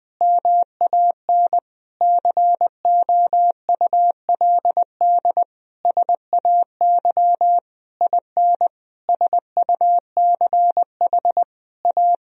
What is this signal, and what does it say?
Morse code, heard as a keyed tone, about 21 wpm, 700 Hz. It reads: MAN COULD SAY IN SUCH A
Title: Sentences from Top 1000 Words - Review Part 2 - 20wpm